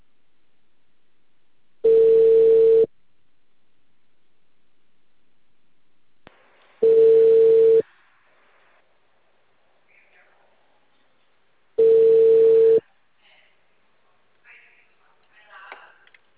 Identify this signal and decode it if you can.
Ringback tone (China, 450 Hz).
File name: out- -2044- - - .36791